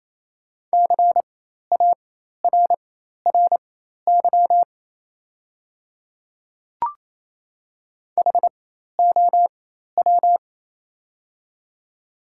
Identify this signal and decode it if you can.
Morse code, heard as a keyed tone, about 27 wpm, spaced as for 13 wpm, 700 Hz. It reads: CARRY HOW
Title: 4x Character Spacing - Top 300 Words - 28wpm